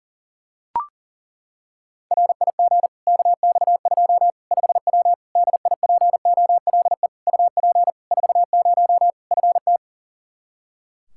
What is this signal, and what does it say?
Morse code, heard as a keyed tone, about 40 wpm, 700 Hz. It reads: RIG KX2 5W DIPOLE UP 40 FT
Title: Practice QSO - Exchange 2 - Info - 40wpm